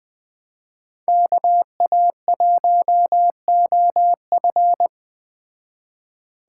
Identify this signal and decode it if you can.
Morse code, heard as a keyed tone, about 20 wpm, 700 Hz. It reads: KA1OF
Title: Speed Racing - USA 2x2 Call Signs 17wpm